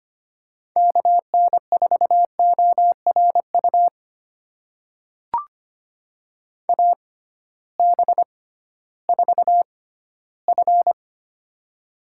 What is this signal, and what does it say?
Morse code, heard as a keyed tone, about 25 wpm, spaced as for 14 wpm, 700 Hz. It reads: KN4ORU AB4F
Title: USA Call Signs - 6x Character Spacing - 25wpm